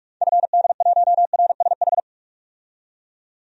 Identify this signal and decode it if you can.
Morse code, heard as a keyed tone, about 41 wpm, 700 Hz. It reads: FD1RSH
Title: DXCC-100 Call Signs - 6 Characters - 45wpm